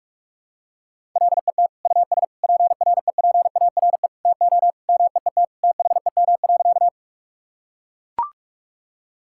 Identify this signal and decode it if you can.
Morse code, heard as a keyed tone, about 42 wpm, 700 Hz. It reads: LET US PREPARE TO MEET THEM.
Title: Sentences from Top 1000 Words - Review Part 6 - 45wpm